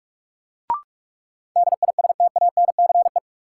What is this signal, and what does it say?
Morse code, heard as a keyed tone, about 46 wpm, 700 Hz. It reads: DISTANCE
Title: Sentences from Top 1200 Words - New Word Drill - 45wpm